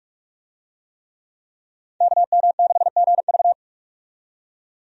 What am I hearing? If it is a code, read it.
Morse code, heard as a keyed tone, about 44 wpm, 700 Hz. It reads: KM6GV
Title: USA Call Signs - 5 Characters - 45wpm